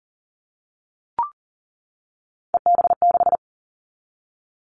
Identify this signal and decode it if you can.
Morse code, heard as a keyed tone, about 40 wpm, 700 Hz. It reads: EB6
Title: Callsign Prefixes for Top 50 DXCC - 40wpm